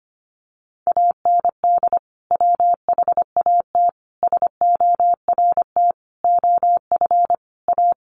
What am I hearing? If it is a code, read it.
Morse code, heard as a keyed tone, about 25 wpm, 700 Hz. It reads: AND WHAT SORT OF A
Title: Sentences from Top 1300 Words 25wpm